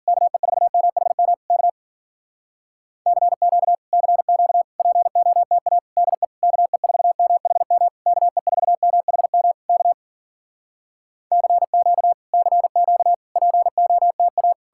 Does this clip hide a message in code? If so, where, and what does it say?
Yes: Morse code from 0.1 to 14.5 s, heard as a keyed tone, about 52 wpm, 700 Hz. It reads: KE4MHM K CQ CQ POTA DE KE4MHM KE4MHM K CQ CQ POTA